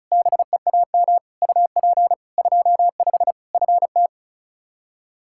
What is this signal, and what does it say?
Morse code, heard as a keyed tone, about 36 wpm, 700 Hz. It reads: BEAM UP 25 FT
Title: Practice QSO - Exchange 2 - Info - 35wpm